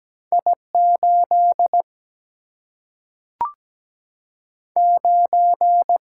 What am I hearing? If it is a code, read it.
Morse code, heard as a keyed tone, about 17 wpm, 700 Hz. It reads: I 8 9
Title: Single-Letter Single-Number 17wpm